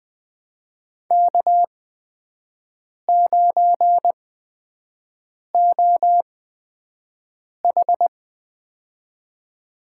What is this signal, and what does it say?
Morse code, heard as a keyed tone, about 20 wpm, spaced as for 5 wpm, 700 Hz. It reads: K9OH
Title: USA Call Signs - 4 Characters - 8x Character Spacing - 20wpm